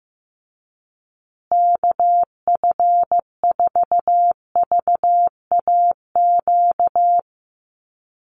Morse code, heard as a keyed tone, about 15 wpm, 700 Hz. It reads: KF4VAQ
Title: USA Call Signs Repeated 3x 15wpm